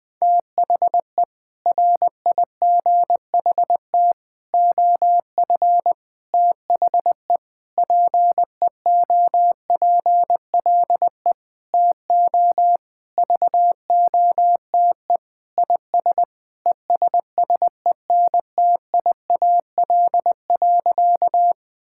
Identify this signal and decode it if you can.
Morse code, heard as a keyed tone, about 20 wpm, 700 Hz. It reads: THE RIGHT OF THE PEOPLE TO VOTE IS ESSENTIAL.
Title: Sentences with 6-Word Phrases - 20wpm